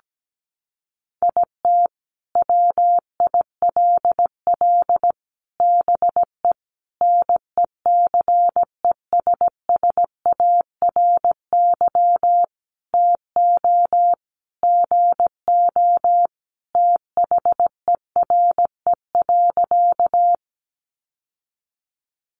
Morse code, heard as a keyed tone, about 17 wpm, 700 Hz. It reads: IT WILL BE NECESSARY TO GO THERE.
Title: Sentences from Top 800 Words 17wpm